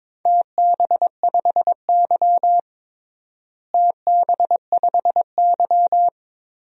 Morse code, heard as a keyed tone, about 22 wpm, 700 Hz. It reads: TB5Y TB5Y
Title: DXCC-100 Call Signs - 4 Characters - Repeated 3x - 22wpm